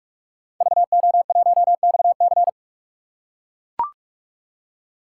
Morse code, heard as a keyed tone, about 43 wpm, 700 Hz. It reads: VO1XC
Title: K1USN SST QSO Practice - 45wpm